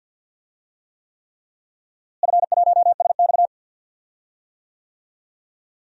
Morse code, heard as a keyed tone, about 51 wpm, 700 Hz. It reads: F1SX
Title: DXCC-100 Call Signs - 4 Characters - Repeated 3x - 50wpm